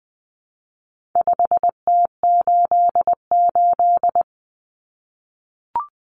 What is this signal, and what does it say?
Morse code, heard as a keyed tone, about 20 wpm, 700 Hz. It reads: 5T88
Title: Common Cut Numbers - 4-Digit 20wpm